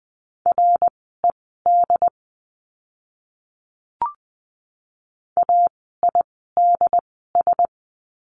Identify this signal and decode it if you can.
Morse code, heard as a keyed tone, about 20 wpm, spaced as for 14 wpm, 700 Hz. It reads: RED AIDS
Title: 068 Lesson 12f - Letter d - Words 20wpm